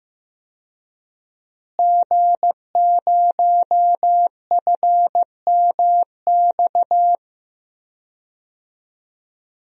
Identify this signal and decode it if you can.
Morse code, heard as a keyed tone, about 15 wpm, 700 Hz. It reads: G0FMX
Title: DXCC-100 Call Signs - 5 Characters - Repeated 2x - 15wpm